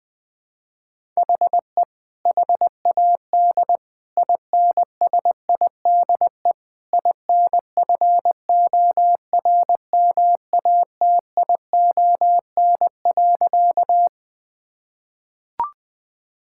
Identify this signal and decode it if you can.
Morse code, heard as a keyed tone, about 21 wpm, 700 Hz. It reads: HE HAD INSIDE INFORMATION.
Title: Sentences from Top 1000 Words - Review Part 6 - 20wpm